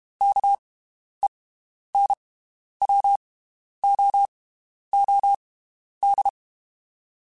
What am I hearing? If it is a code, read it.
Morse code, heard as a keyed tone, about 32 wpm, spaced as for 11 wpm, 800 Hz. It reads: KENWOOD
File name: (spaced) / QSOs Spaced-mp3s / WD0BGP 1 @30 sp